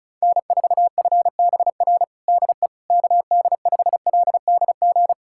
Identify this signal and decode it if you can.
Morse code, heard as a keyed tone, about 34 wpm, 700 Hz. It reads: N4FBR DE KD5LDG
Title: USA Callsign DE Callsign 35wpm